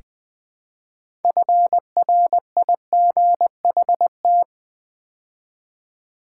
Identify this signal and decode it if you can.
Morse code, heard as a keyed tone, about 21 wpm, 700 Hz. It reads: FRIGHT
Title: Build A Word - 3 Letters At A Time - 8-Letter Words - 20wpm